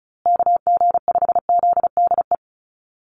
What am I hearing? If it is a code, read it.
Morse code, heard as a keyed tone, about 34 wpm, 700 Hz. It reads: KG5ZDE
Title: USA Call Signs Repeated 3x 35wpm